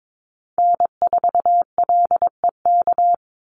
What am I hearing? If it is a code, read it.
Morse code, heard as a keyed tone, about 22 wpm, 700 Hz. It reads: N4LEK
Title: USA Call Signs 22wpm